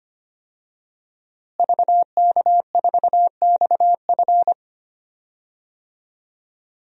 Morse code, heard as a keyed tone, about 25 wpm, 700 Hz. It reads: VK4XF
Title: DXCC-100 Call Signs - 5 Characters - Repeated 2x - 25wpm